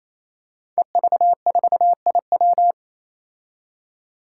Morse code, heard as a keyed tone, about 27 wpm, 700 Hz. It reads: EV4IW
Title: Five Challenging Letter-Number Combinations 28wpm